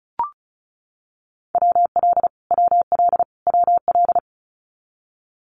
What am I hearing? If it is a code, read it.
Morse code, heard as a keyed tone, about 34 wpm, 700 Hz. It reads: WL WL WL
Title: Abbreviations Repeated 3x - 35wpm